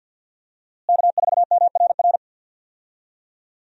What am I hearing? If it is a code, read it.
Morse code, heard as a keyed tone, about 53 wpm, 700 Hz. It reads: K4MRR
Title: USA Call Signs - 5 Characters - Repeated 3x - 50wpm